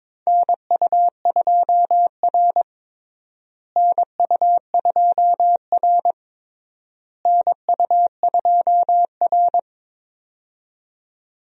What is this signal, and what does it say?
Morse code, heard as a keyed tone, about 21 wpm, 700 Hz. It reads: NU2R NU2R NU2R
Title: USA Call Signs - 4 Characters - Repeated 3x - 22wpm